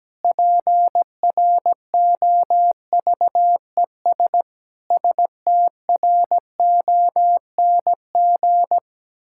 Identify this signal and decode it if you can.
Morse code, heard as a keyed tone, about 17 wpm, 700 Hz. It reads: PROVES STRONG
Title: Pangram Challenge v2 - 17wpm